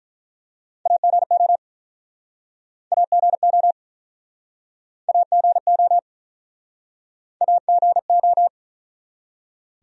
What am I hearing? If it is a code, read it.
Morse code, heard as a keyed tone, about 47 wpm, 700 Hz. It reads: AGO AGO AGO AGO